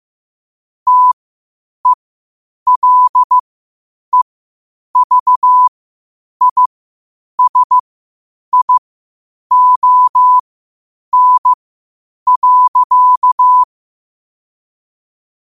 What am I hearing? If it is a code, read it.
Morse code, heard as a keyed tone, about 15 wpm, spaced as for 8 wpm, 1000 Hz. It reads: TELEVISION.
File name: qotdfiles / 8wpm